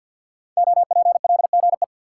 Morse code, heard as a keyed tone, about 46 wpm, 700 Hz. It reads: OPLGE
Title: Code Groups - 5 Letters - 50wpm